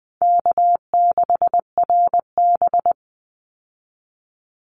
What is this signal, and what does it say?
Morse code, heard as a keyed tone, about 20 wpm, 700 Hz. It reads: K6RB
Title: Call Signs Repeated 2x - 20wpm